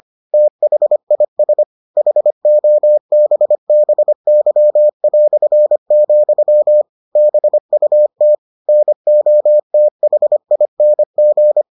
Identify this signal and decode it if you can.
Morse code, heard as a keyed tone, about 25 wpm, 600 Hz. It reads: THIS HOBBY", BUT NOTHING